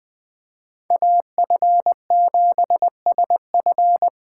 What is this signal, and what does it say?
Morse code, heard as a keyed tone, about 20 wpm, 700 Hz. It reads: AF7SF
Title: Speed Racing - USA 2x2 Call Signs 17wpm